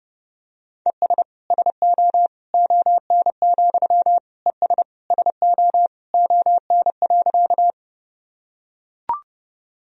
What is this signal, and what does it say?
Morse code, heard as a keyed tone, about 30 wpm, 700 Hz. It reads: ES SO ON, ES SO ON.
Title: Sentences from Top 100 Words and Abbreviations 30wpm